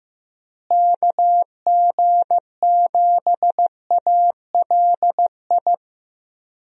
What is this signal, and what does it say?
Morse code, heard as a keyed tone, about 16 wpm, 700 Hz. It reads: KG7ALI
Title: USA Call Signs - 6 Characters - Repeated 2x - 15wpm